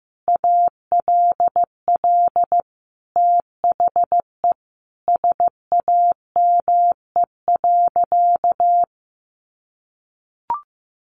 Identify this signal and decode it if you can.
Morse code, heard as a keyed tone, about 15 wpm, 700 Hz. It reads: ALL THE SAME.
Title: Sentences from Top 1500 Words 15wpm